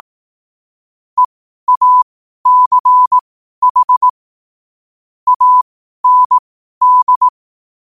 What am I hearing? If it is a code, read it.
Morse code, heard as a keyed tone, about 18 wpm, spaced as for 12 wpm, 1000 Hz. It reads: EACH AND
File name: qotdfiles / 12wpm